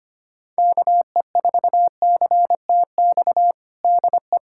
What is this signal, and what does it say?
Morse code, heard as a keyed tone, about 25 wpm, 700 Hz. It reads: KE4CTX DE
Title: USA Callsign DE Callsign - 6 Characters - 25wpm